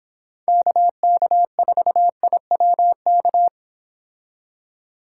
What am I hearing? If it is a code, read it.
Morse code, heard as a keyed tone, about 27 wpm, 700 Hz. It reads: KK4IWK
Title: Speed Racing - USA 2x3 Call Signs 22wpm